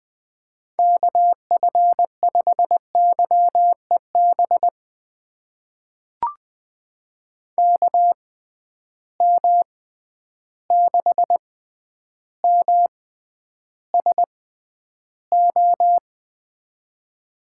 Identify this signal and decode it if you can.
Morse code, heard as a keyed tone, about 20 wpm, spaced as for 10 wpm, 700 Hz. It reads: KF5YEB KM6MSO
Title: USA Call Signs - 6 Characters - 6x Character Spacing - 20wpm